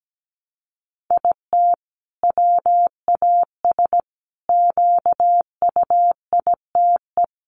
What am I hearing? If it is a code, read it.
Morse code, heard as a keyed tone, about 17 wpm, 700 Hz. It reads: IT WAS QUITE
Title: Sentences from Top 700 Words 17wpm